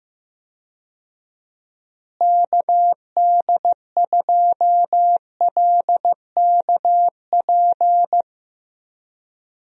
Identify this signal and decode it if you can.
Morse code, heard as a keyed tone, about 15 wpm, 700 Hz. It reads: KD2LKP
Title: USA Call Signs - 6 Characters - Repeated 3x - 15wpm